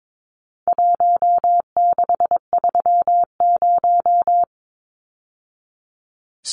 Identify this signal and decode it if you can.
Morse code, heard as a keyed tone, about 22 wpm, 700 Hz. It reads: 1630
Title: Numbers - Contest Style - 4-Digit 22wpm